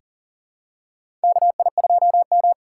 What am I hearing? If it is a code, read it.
Morse code, heard as a keyed tone, about 40 wpm, 700 Hz. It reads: KI2M
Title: USA Call Signs - 6x Character Spacing - 40wpm